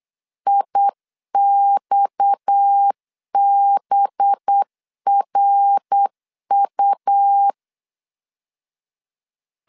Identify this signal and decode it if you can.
Morse code, heard as a keyed tone, about 8 wpm, 800 Hz. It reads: IXBRU
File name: LEZIONI / 6 PKX0 / 6 PKX0 8wpm 3341